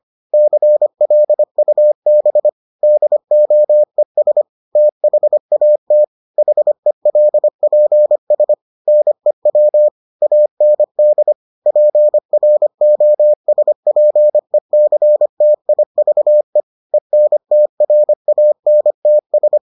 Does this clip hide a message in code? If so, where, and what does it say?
Yes: Morse code from 0.3 to 19.6 s, heard as a keyed tone, about 25 wpm, 600 Hz. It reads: CLUB DOES THAT HELPS NEW AND PROSPECTIVE ENTRANTS